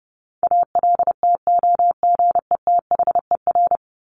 Morse code, heard as a keyed tone, about 30 wpm, 700 Hz. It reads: ALTOGETHER
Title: Top 1501-1750 Words 30wpm